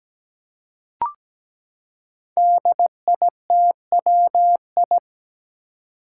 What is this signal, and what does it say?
Morse code, heard as a keyed tone, about 17 wpm, 700 Hz. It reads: DITWI
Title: Five Challenging Letter-Number Combinations 17wpm